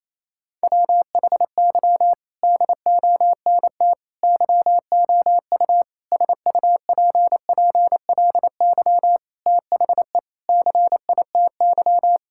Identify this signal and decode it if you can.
Morse code, heard as a keyed tone, about 28 wpm, 700 Hz. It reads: WHY DONT YOU SUPPLY THE CITY
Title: Sentences from Top 1000 Words - Review Part 5 - 28wpm